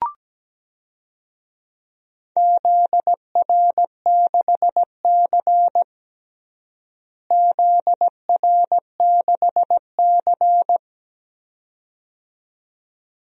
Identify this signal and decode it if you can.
Morse code, heard as a keyed tone, about 17 wpm, 700 Hz. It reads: ZR6C ZR6C
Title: DXCC-100 Call Signs - 4 Characters - Repeated 2x - 17wpm